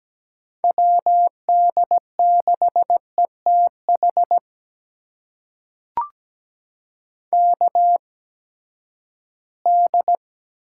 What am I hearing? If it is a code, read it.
Morse code, heard as a keyed tone, about 17 wpm, spaced as for 12 wpm, 700 Hz. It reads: WD6ETH KD
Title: USA Call Signs - 6 Characters - 8x Character Spacing - 17wpm